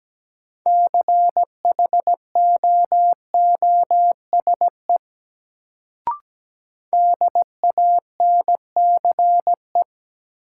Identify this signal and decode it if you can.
Morse code, heard as a keyed tone, about 17 wpm, 700 Hz. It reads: CHOOSE DANCE
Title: Sentences from Top 700 Words - New Word Drill - 17wpm